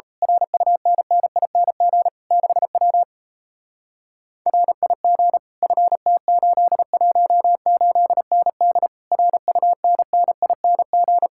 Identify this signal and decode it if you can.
Morse code, heard as a keyed tone, about 33 wpm, 700 Hz. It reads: RUNNING 6W RIG FT818ND RUNNING
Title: Speed-Racing Common QSO Phrases - 28wpm